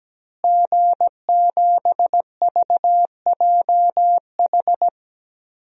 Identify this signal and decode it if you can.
Morse code, heard as a keyed tone, about 17 wpm, 700 Hz. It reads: G7VJH
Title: DXCC-100 Call Signs - 5 Characters - Repeated 3x - 17wpm